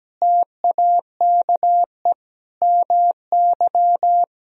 Morse code, heard as a keyed tone, about 17 wpm, 700 Hz. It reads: TAKE MY
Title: Sentences from Top 1000 Words - Review Part 3 - 17wpm